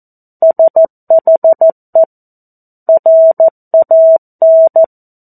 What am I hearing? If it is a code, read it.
Morse code, heard as a keyed tone, about 14 wpm, 650 Hz. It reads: SHE RAN
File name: 02-Chapter-02 14 WPM